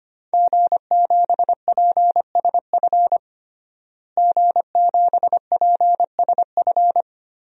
Morse code, heard as a keyed tone, about 25 wpm, 700 Hz. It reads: G7PSF G7PSF
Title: DXCC-100 Call Signs - 5 Characters - Repeated 3x - 25wpm